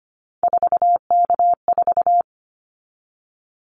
Morse code, heard as a keyed tone, about 25 wpm, 700 Hz. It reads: 4K4
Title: Callsign Prefixes for Top 100 DXCC - 25wpm